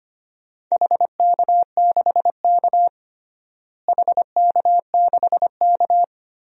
Morse code, heard as a keyed tone, about 25 wpm, 700 Hz. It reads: HK6K HK6K
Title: DXCC-100 Call Signs - 4 Characters - Repeated 2x - 25wpm